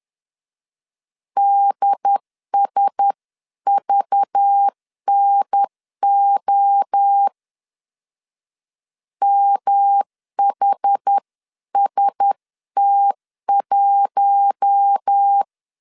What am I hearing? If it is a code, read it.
Morse code, heard as a keyed tone, about 10 wpm, 800 Hz. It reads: DSVNO MHST1
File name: CW / DET / 1 6 1500C S C 10wpm 4252